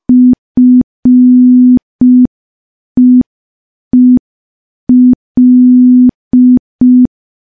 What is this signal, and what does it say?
Morse code, heard as a keyed tone, about 5 wpm, 250 Hz. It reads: FEEL
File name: morse / morse-training-05WPM